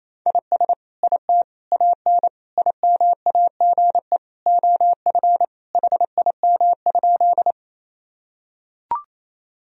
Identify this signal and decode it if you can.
Morse code, heard as a keyed tone, about 28 wpm, 700 Hz. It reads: IS IT AN IMAGE OF HIM?
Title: Sentences from Top 1000 Words - Review Part 4 - 28wpm